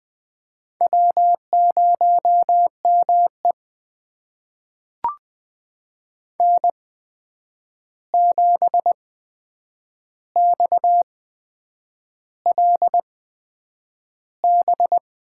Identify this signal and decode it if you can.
Morse code, heard as a keyed tone, about 20 wpm, spaced as for 8 wpm, 700 Hz. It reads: W0ME N7XLB
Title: USA Call Signs - 8x Character Spacing - 20wpm